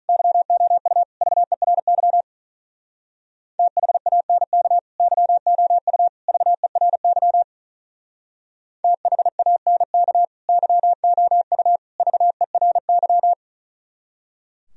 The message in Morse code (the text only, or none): YOU VERY THANK YOU VERY THANK YOU VERY